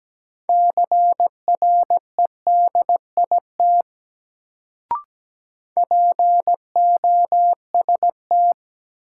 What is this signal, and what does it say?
Morse code, heard as a keyed tone, about 17 wpm, 700 Hz. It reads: CREDIT POST
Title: Sentences from Top 1000 Words - New Word Drill - 17wpm